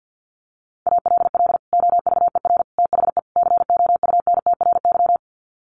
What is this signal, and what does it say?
Morse code, heard as a keyed tone, about 50 wpm, 700 Hz. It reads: ALL OVER THE COUNTRY
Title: Sets of 4 Words 50wpm